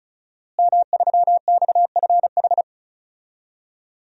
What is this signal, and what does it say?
Morse code, heard as a keyed tone, about 34 wpm, 700 Hz. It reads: M3XFH
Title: DXCC-100 Call Signs - 5 Characters - Repeated 2x - 35wpm